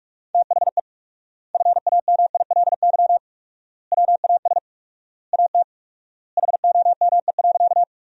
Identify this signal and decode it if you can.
Morse code, heard as a keyed tone, about 46 wpm, 700 Hz. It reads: THE FAMILY WAS AT HOME.
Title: 4x Word-Spacing - Sentences from Top 400 Words - 45wpm